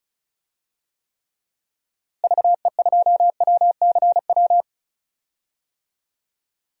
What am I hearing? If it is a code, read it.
Morse code, heard as a keyed tone, about 36 wpm, 700 Hz. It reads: VE2WCW
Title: DXCC-100 Call Signs - 6 Characters - Repeated 3x - 35wpm